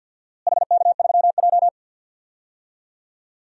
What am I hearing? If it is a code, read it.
Morse code, heard as a keyed tone, about 51 wpm, 700 Hz. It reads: HK3J
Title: DXCC-100 Call Signs - 4 Characters - Repeated 3x - 50wpm